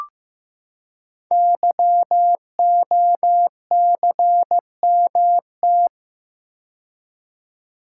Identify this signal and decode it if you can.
Morse code, heard as a keyed tone, about 15 wpm, 700 Hz. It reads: YOCMT
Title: Code Groups - 5 Letters - 15wpm